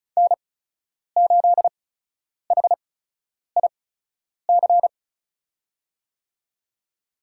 Morse code, heard as a keyed tone, about 34 wpm, spaced as for 10 wpm, 700 Hz. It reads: N8HIC
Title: USA Call Signs - 5 Characters - 8x Character Spacing - 35wpm